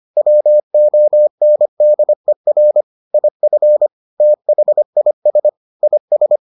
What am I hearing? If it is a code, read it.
Morse code, heard as a keyed tone, about 25 wpm, 600 Hz. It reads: WONDER IF THIS IS